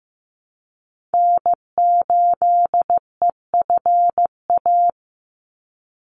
Morse code, heard as a keyed tone, about 15 wpm, 700 Hz. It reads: N8EFA
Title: USA Call Signs 15wpm